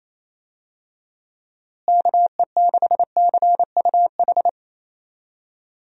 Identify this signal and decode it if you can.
Morse code, heard as a keyed tone, about 27 wpm, 700 Hz. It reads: KE6CUH
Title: USA Call Signs - 6 Characters - Repeated 3x - 28wpm